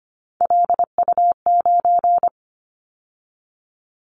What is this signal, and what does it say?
Morse code, heard as a keyed tone, about 25 wpm, 700 Hz. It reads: LU9
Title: Callsign Prefixes for Top 50 DXCC - 25wpm